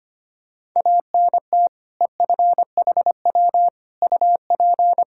Morse code, heard as a keyed tone, about 25 wpm, 700 Hz. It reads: ANT EFHW UP
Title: Practice QSO - Exchange 2 - Info - 25wpm